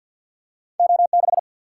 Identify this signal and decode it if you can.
Morse code, heard as a keyed tone, about 50 wpm, 700 Hz. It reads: O 6
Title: Single-Letter Single-Number 50wpm